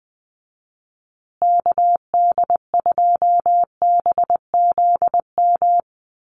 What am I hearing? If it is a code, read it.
Morse code, heard as a keyed tone, about 20 wpm, 700 Hz. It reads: KD2BZM
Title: USA Call Signs Repeated 2x 20wpm